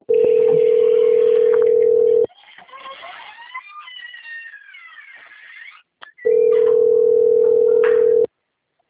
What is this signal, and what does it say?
Ringback tone (North America), 440 + 480 Hz.
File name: out- - - - - .33645